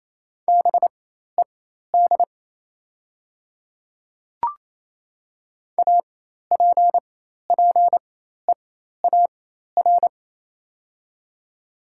Morse code, heard as a keyed tone, about 28 wpm, spaced as for 13 wpm, 700 Hz. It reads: BED APPEAR
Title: 4x Character Spacing - Top 500 Words - 28wpm